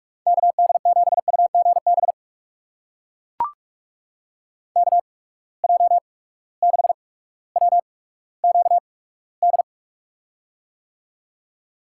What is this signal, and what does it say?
Morse code, heard as a keyed tone, about 42 wpm, spaced as for 19 wpm, 700 Hz. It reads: KD7UGB KJ6WQD
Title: USA Call Signs - 6 Characters - 8x Character Spacing - 45wpm